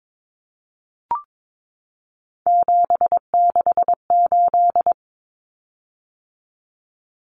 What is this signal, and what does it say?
Morse code, heard as a keyed tone, about 22 wpm, 700 Hz. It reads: 768
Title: Common Cut Numbers - 3-Digit 22wpm